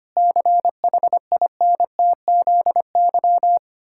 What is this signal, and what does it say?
Morse code, heard as a keyed tone, about 25 wpm, 700 Hz. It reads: CHINTZY
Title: Pangram Challenge v2 - 25wpm